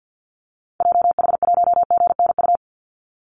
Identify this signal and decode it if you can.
Morse code, heard as a keyed tone, about 50 wpm, 700 Hz. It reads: JH1GNU
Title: Call Signs - 50wpm